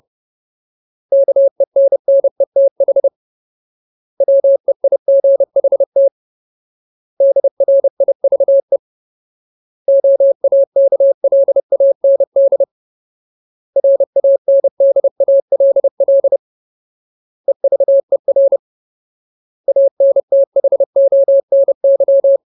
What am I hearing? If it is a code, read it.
Morse code, heard as a keyed tone, about 30 wpm, 550 Hz. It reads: KENNETH WEIGHT DRIVE OAKLAND RANDALL EVER ANTHONY
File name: Ultimate Word List / 30WPM Ultimate